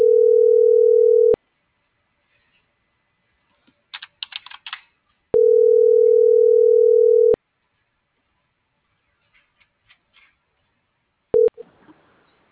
Ringback tone (North America), 440 + 480 Hz.